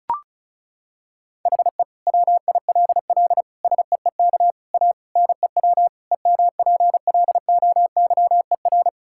Morse code, heard as a keyed tone, about 34 wpm, 700 Hz. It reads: HE WILL SEEK A NEW EMPLOYER
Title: Sentences from Top 1600 Words 35wpm